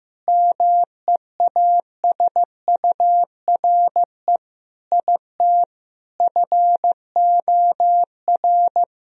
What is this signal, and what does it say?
Morse code, heard as a keyed tone, about 16 wpm, 700 Hz. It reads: MEASURE IT FOR
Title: Sentences from Top 1000 Words - Review Part 2 - 15wpm